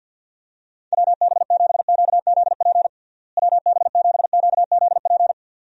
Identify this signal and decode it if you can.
Morse code, heard as a keyed tone, about 52 wpm, 700 Hz. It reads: WB7QZP WB7QZP
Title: USA Call Signs - 6 Characters - Repeated 2x - 50wpm